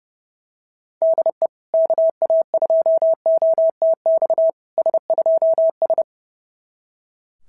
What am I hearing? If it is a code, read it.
Morse code, heard as a keyed tone, about 30 wpm, 660 Hz. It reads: DE KA2OTX S2S
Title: SOTA QSO Practice 30wpm